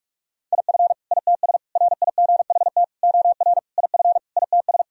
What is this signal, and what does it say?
Morse code, heard as a keyed tone, about 46 wpm, 700 Hz. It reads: IF ITS RIGHT OR IF ITS
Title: Sentences from Top 1000 Words - Review Part 3 - 45wpm